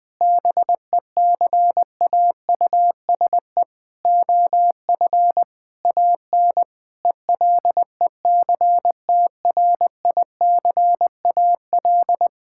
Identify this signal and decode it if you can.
Morse code, heard as a keyed tone, about 21 wpm, 700 Hz. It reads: BECAUSE OF AN ELECTRICAL
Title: Sentences from Top 1000 Words - Review Part 6 - 20wpm